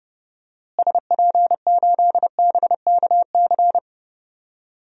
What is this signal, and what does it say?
Morse code, heard as a keyed tone, about 30 wpm, 700 Hz. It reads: SP8BKC
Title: DXCC-100 Call Signs - 6 Characters - Repeated 2x - 30wpm